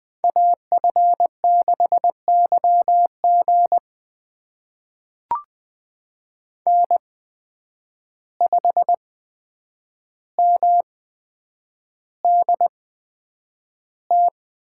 Morse code, heard as a keyed tone, about 20 wpm, spaced as for 9 wpm, 700 Hz. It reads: AF6YG N5MDT
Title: USA Call Signs - 5 Characters - 8x Character Spacing - 20wpm